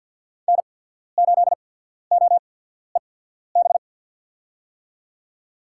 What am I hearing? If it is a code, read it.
Morse code, heard as a keyed tone, about 53 wpm, spaced as for 14 wpm, 700 Hz. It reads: N8OEB
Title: USA Call Signs - 5 Characters - 8x Character Spacing - 50wpm